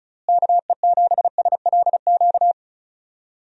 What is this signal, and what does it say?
Morse code, heard as a keyed tone, about 36 wpm, 700 Hz. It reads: KE7SLQ
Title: USA Call Signs - 6 Characters - Repeated 3x - 35wpm